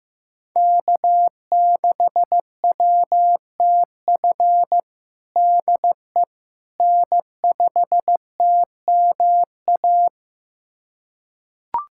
Morse code, heard as a keyed tone, about 15 wpm, 700 Hz. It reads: K6WTF DE N5TMA
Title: USA Callsign DE Callsign - 5 Characters - 15wpm